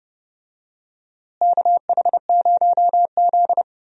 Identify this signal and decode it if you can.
Morse code, heard as a keyed tone, about 30 wpm, 700 Hz. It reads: KH0Z
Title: USA Call Signs - 4 Characters - Repeated 2x - 30wpm